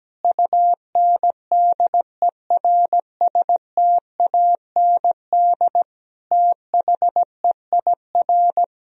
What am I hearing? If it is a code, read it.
Morse code, heard as a keyed tone, about 17 wpm, 700 Hz. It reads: UNDERSTAND THEIR
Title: Sentences from Top 1000 Words - Review Part 3 - 17wpm